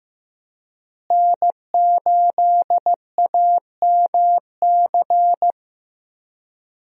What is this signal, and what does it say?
Morse code, heard as a keyed tone, about 15 wpm, 700 Hz. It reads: N8AMC
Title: USA Call Signs - 5 Characters - Repeated 3x - 15wpm